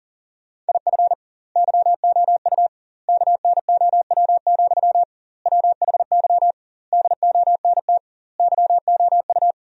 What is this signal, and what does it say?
Morse code, heard as a keyed tone, about 40 wpm, 700 Hz. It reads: IF YOU KNOW, WHY DONT YOU
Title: 2x Word-Spacing - Sentences from Top 300 Words - 40wpm